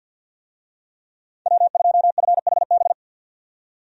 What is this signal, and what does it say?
Morse code, heard as a keyed tone, about 52 wpm, 700 Hz. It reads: W2FHB
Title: USA Call Signs - 5 Characters - 50wpm